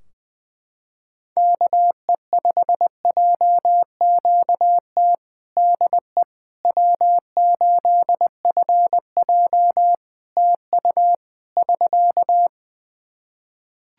Morse code, heard as a keyed tone, about 20 wpm, 700 Hz. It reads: KE5JQT DE W8FJ TU <SK>
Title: Practice QSO - Exchange 4 - Ending - 20wpm